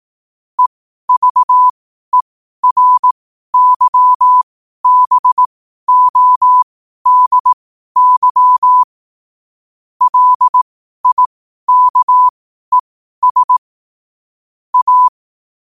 Morse code, heard as a keyed tone, about 18 wpm, spaced as for 12 wpm, 1000 Hz. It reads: EVERYBODY LIKES A
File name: qotdfiles / 12wpm